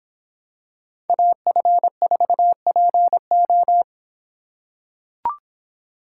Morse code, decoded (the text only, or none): AF4PO